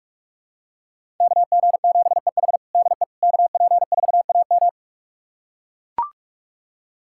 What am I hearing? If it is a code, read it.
Morse code, heard as a keyed tone, about 44 wpm, 700 Hz. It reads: KG7EH DE KP4AM
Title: USA Callsign DE Callsign - 5 Characters - 45wpm